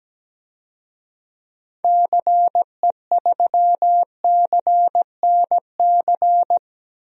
Morse code, heard as a keyed tone, about 17 wpm, 700 Hz. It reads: CE3CNC
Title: DXCC-100 Call Signs - 6 Characters - Repeated 3x - 17wpm